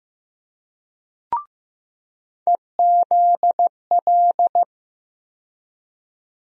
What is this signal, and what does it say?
Morse code, heard as a keyed tone, about 15 wpm, 700 Hz. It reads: EZL
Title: USA Call Sign Postfixes - 3 Letters 15wpm